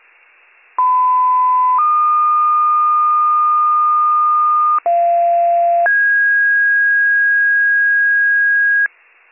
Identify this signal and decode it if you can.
Two-tone page, heard as a one-second tone, then a three-second tone, about 1010 then 1220 Hz, about 690 then 1650 Hz, with noise. It is heard from 0.8 to 8.9 s.